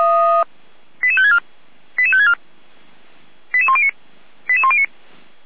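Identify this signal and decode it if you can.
Five-tone selective call (ZVEI), heard as quick runs of five tones, tones of about 70 ms. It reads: 88564 88564 88109 88109
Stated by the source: ZVEI five-tone selective call received on an SDR.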